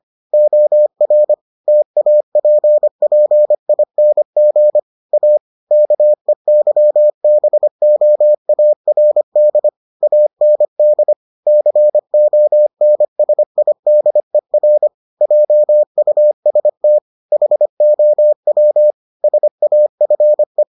Morse code, heard as a keyed tone, about 25 wpm, 600 Hz. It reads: OR TAPPING A KEYBOARD AND CONSIDER JUST HOW SAFE